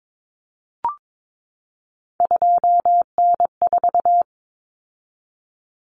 Morse code, heard as a keyed tone, about 22 wpm, 700 Hz. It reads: 2N4
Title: Common Cut Numbers - 3-Digit 22wpm